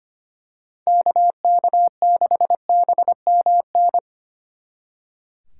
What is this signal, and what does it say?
Morse code, heard as a keyed tone, about 29 wpm, 700 Hz. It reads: KK6BMN
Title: Speed Racing - USA 2x3 Call Signs 25wpm